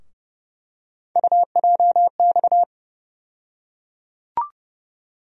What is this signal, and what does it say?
Morse code, heard as a keyed tone, about 29 wpm, 700 Hz. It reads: UJX
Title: USA Call Sign Postfixes - 3 Letters 30wpm